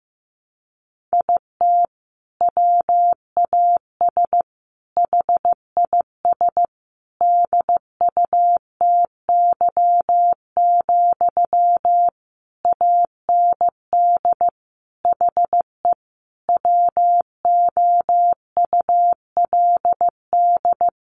Morse code, heard as a keyed tone, about 15 wpm, 700 Hz. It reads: IT WAS HIS DUTY, AND HE WOULD
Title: Sentences from Top 1500 Words 15wpm